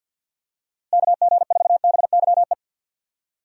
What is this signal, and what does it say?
Morse code, heard as a keyed tone, about 46 wpm, 700 Hz. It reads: KG4BYE
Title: USA Call Signs - 6 Characters - Repeated 3x - 50wpm